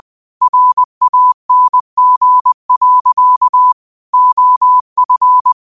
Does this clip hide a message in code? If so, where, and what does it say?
Yes: Morse code from 0.4 to 5.5 s, heard as a keyed tone, about 20 wpm, 1000 Hz. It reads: RANG. OF